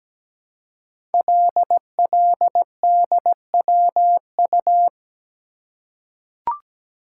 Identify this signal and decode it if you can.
Morse code, heard as a keyed tone, about 17 wpm, 700 Hz. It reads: LLDWU
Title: Code Groups - 5 Letter:Numbers - 17wpm